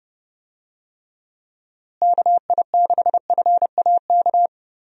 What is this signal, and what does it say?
Morse code, heard as a keyed tone, about 31 wpm, 700 Hz. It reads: KI6FAK
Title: USA Call Signs - 6 Characters - Repeated 3x - 30wpm